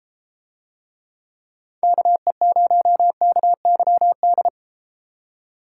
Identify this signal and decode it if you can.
Morse code, heard as a keyed tone, about 33 wpm, 700 Hz. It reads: KE0KYD